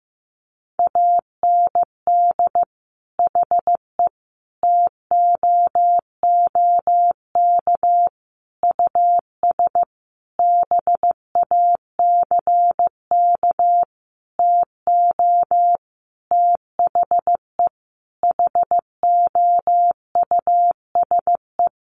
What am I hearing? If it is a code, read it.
Morse code, heard as a keyed tone, about 14 wpm, 700 Hz. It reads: AND HE TOOK US BACK TO THE HOUSE
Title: Sentences from Top 1500 Words 15wpm